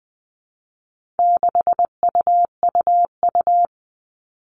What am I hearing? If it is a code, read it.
Morse code, heard as a keyed tone, about 20 wpm, 700 Hz. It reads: 6UUU
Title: All Cut Numbers - 4-Digit 20wpm